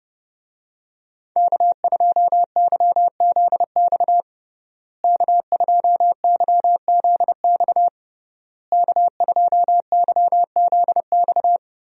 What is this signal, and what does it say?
Morse code, heard as a keyed tone, about 30 wpm, 700 Hz. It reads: K2YZX K2YZX K2YZX
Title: USA Call Signs - 5 Characters - Repeated 3x - 30wpm